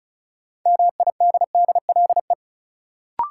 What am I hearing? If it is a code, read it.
Morse code, heard as a keyed tone, about 36 wpm, 700 Hz. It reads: MIDDLE
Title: Sets of 2 Words - New Word Drill - 35wpm